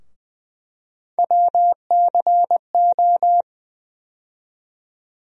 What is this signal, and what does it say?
Morse code, heard as a keyed tone, about 20 wpm, 700 Hz. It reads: WCO
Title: USA Call Sign Postfixes - 3 Letters 20wpm